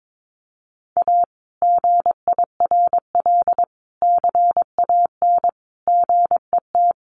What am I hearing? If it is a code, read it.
Morse code, heard as a keyed tone, about 22 wpm, 700 Hz. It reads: A GIRL CAN GET
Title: Sentences from Top 1400 Words 22wpm